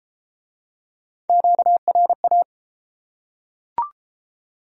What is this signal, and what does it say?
Morse code, heard as a keyed tone, about 31 wpm, 700 Hz. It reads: QRA